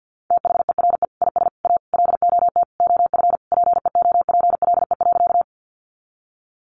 Morse code, heard as a keyed tone, about 50 wpm, 700 Hz. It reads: THERE IS A LOT OF PEOPLE.
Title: Sentences from Top 500 Words 50wpm